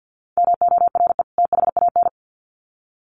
Morse code, heard as a keyed tone, about 50 wpm, 700 Hz. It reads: MORE THAN
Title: Sets of 2 Words 50wpm